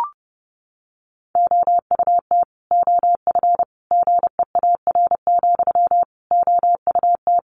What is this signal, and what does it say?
Morse code, heard as a keyed tone, about 30 wpm, 700 Hz. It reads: OUT OF GEAR, OUT
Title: Sentences from Top 1400 Words 30wpm